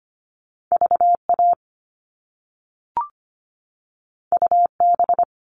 Morse code, heard as a keyed tone, about 25 wpm, 700 Hz. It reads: V A U B
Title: All Cut Numbers - 2-Digit 25wpm